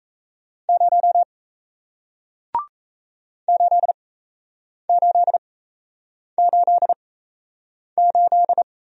Morse code, heard as a keyed tone, about 37 wpm, spaced as for 8 wpm, 700 Hz. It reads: 0 8888